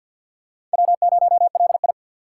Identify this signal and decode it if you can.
Morse code, heard as a keyed tone, about 46 wpm, 700 Hz. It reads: W0LI
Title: USA Call Signs - 3x Character Spacing - 50wpm